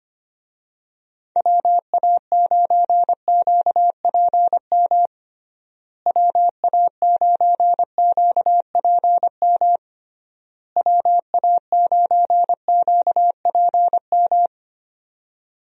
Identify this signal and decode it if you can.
Morse code, heard as a keyed tone, about 25 wpm, 700 Hz. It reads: WA9QPM WA9QPM WA9QPM
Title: USA Call Signs - 6 Characters - Repeated 3x - 25wpm